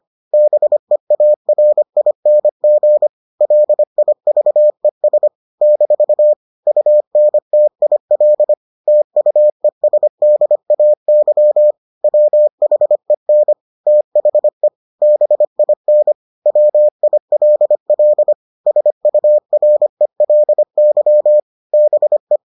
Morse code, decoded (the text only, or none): BEARING LIVES - UNTIL TUESDAY WHEN THE BIN WILL SURELY BE